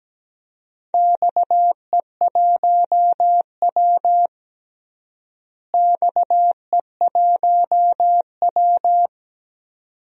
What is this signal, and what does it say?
Morse code, heard as a keyed tone, about 17 wpm, 700 Hz. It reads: XE1W XE1W
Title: DXCC-100 Call Signs - 4 Characters - Repeated 2x - 17wpm